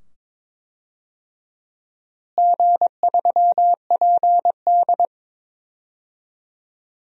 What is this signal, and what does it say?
Morse code, heard as a keyed tone, about 22 wpm, 700 Hz. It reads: G3PD
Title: DXCC-100 Call Signs - 4 Characters - Repeated 2x - 22wpm